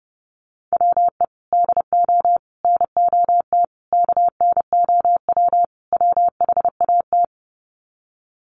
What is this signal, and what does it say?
Morse code, heard as a keyed tone, about 30 wpm, 700 Hz. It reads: WE DO NOT KNOW WHAT
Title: Sets of 5 Words 30wpm